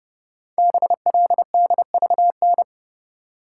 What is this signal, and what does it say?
Morse code, heard as a keyed tone, about 30 wpm, 700 Hz. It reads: BLDVN
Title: Code Groups - 5 Letter:Numbers - 30wpm